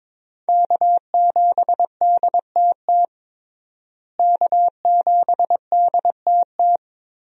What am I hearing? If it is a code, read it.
Morse code, heard as a keyed tone, about 22 wpm, 700 Hz. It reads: K7DTT K7DTT
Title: USA Call Signs - 5 Characters - Repeated 3x - 22wpm